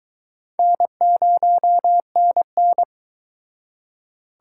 Morse code, heard as a keyed tone, about 23 wpm, 700 Hz. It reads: N0NN